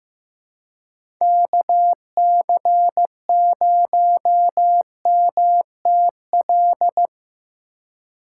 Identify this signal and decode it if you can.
Morse code, heard as a keyed tone, about 15 wpm, 700 Hz. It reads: KC0MTL
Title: USA Call Signs - 6 Characters - 4x Character Spacing - 15wpm